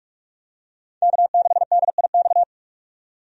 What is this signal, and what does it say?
Morse code, heard as a keyed tone, about 43 wpm, 700 Hz. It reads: K6DIX
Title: USA Call Signs - 5 Characters - 45wpm